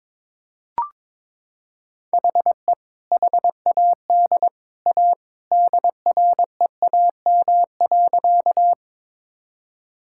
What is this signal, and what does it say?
Morse code, heard as a keyed tone, about 22 wpm, 700 Hz. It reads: HE HAD A DREAM.
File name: practice / 22wpm / Sentences with 2-Word Phrases - 22wpm